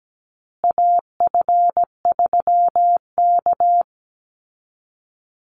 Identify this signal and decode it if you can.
Morse code, heard as a keyed tone, about 17 wpm, 700 Hz. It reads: AF3K
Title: Call Signs Repeated 3x - 17wpm